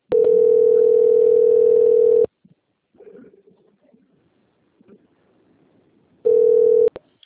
Ringback tone (North America), 440 + 480 Hz.